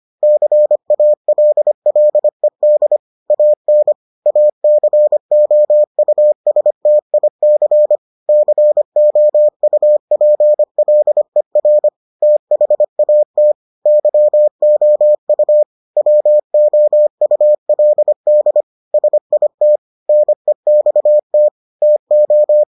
Morse code, heard as a keyed tone, about 25 wpm, 600 Hz. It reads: CALLED AN ACOUSTIC COUPLER THAT YOU WOULD SIT NEXT TO